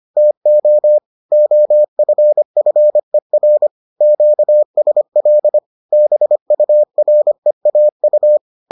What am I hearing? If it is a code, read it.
Morse code, heard as a keyed tone, about 25 wpm, 600 Hz. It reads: TO OFFER QSL BUREAU